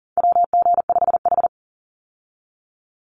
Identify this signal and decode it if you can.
Morse code, heard as a keyed tone, about 40 wpm, 700 Hz. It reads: WG5H
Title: Call Signs - 40wpm